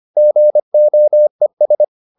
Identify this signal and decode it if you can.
Morse code, heard as a keyed tone, about 25 wpm, 600 Hz. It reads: GOES